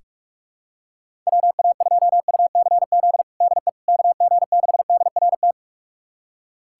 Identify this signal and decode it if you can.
Morse code, heard as a keyed tone, about 46 wpm, 700 Hz. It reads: WA2UCZ DE KG6DRT
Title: USA Callsign DE Callsign - 6 Characters - 45wpm